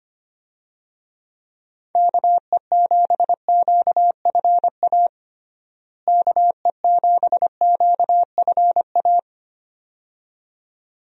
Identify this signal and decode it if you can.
Morse code, heard as a keyed tone, about 25 wpm, 700 Hz. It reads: KE7QFA KE7QFA
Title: USA Call Signs - 6 Characters - Repeated 2x - 25wpm